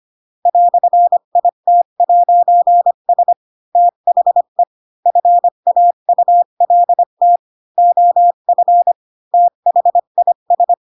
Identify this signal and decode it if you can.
Morse code, heard as a keyed tone, about 25 wpm, 700 Hz. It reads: "IT'S THE FAULT OF THIS